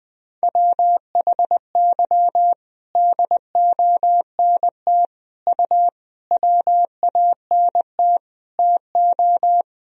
Morse code, heard as a keyed tone, about 20 wpm, 700 Hz. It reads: WHY DONT U WANT TO
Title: Sentences from Top 300 Words and Abbreviations 20wpm